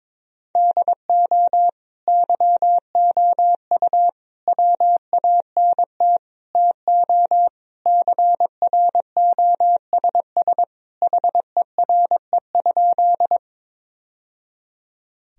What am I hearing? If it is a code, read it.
Morse code, heard as a keyed tone, about 22 wpm, 700 Hz. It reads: DO YOU WANT TO CROSS HERE?
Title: Sentences from Top 1000 Words - Review Part 1 - 22wpm